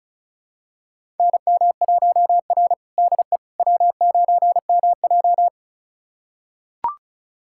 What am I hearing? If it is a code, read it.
Morse code, heard as a keyed tone, about 36 wpm, 700 Hz. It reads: NM1R DE W9MJ
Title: USA Callsign DE Callsign - 4 Characters - 35wpm